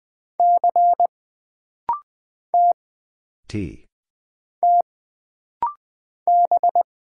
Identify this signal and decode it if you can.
Morse code, heard as a keyed tone, about 20 wpm, 700 Hz. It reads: C T T B
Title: Single Letter-Number Alt - Rapid-Fire 20wpm